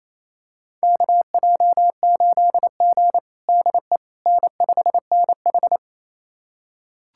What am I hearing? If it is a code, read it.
Morse code, heard as a keyed tone, about 28 wpm, 700 Hz. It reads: KJ8G DE N5NH
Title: USA Callsign DE Callsign - 4 Characters - 28wpm